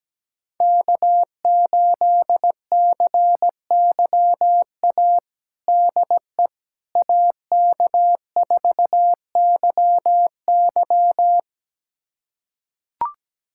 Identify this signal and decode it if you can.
Morse code, heard as a keyed tone, about 17 wpm, 700 Hz. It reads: K8CYA DE AK4YY
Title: USA Callsign DE Callsign - 5 Characters - 17wpm